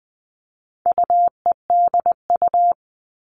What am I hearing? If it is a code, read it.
Morse code, heard as a keyed tone, about 20 wpm, 700 Hz. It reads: UEDU
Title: All Cut Numbers - 4-Digit 20wpm